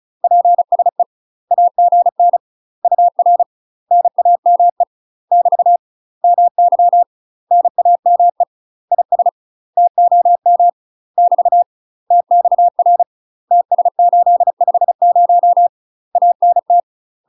Morse code, heard as a keyed tone, about 36 wpm, 700 Hz. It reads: PSE AGN UR NAME <BT> MY NAME IS TOM <BT> TXR TS850 ANT
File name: UK MASTER QSOs TRAINING / 35WPM / UK MASTER QSOs 0011